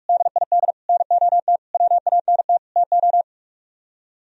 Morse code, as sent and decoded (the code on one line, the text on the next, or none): -.. .. -.. / -. --- - / .-- .- -. - / - ---
DID NOT WANT TO